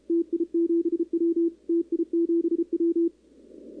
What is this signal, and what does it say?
Morse code, heard as a keyed tone, about 34 wpm, 340 Hz. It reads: TI7W TI7W
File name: contests / 2019-cw-wpx-kt5j-k5pi- -2330 0-11